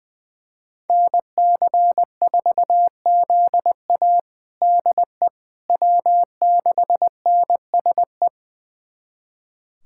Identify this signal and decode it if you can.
Morse code, heard as a keyed tone, about 20 wpm, 700 Hz. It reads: NC4ZA DE W6NSE
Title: USA Callsign DE Callsign - 5 Characters - 20wpm